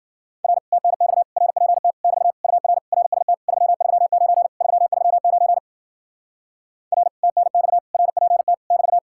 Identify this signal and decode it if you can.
Morse code, heard as a keyed tone, about 52 wpm, 700 Hz. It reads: R TNX RPT <BT> UR RST 339 339 R TNX RPT <BT>